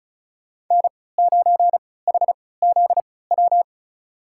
Morse code, heard as a keyed tone, about 36 wpm, spaced as for 20 wpm, 700 Hz. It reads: N9HZW
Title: USA Call Signs - 5 Characters - 3x Character Spacing - 35wpm